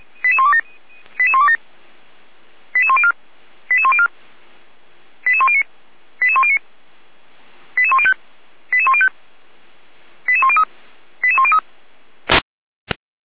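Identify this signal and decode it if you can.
Five-tone selective call (ZVEI), heard as quick runs of five tones, tones of about 70 ms. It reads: 88127 88104 88104 88109 88109 88105 88105 88103 88103
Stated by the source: ZVEI five-tone selective call 88063 received by SDR.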